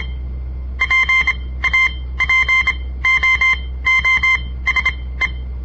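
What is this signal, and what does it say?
Morse code, heard as a keyed tone, about 26 wpm, spaced as for 17 wpm, 2100 Hz. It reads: PAPOOSE
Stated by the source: On-air Morse code practice broadcast by a southern Montana amateur radio association; all code, no read back.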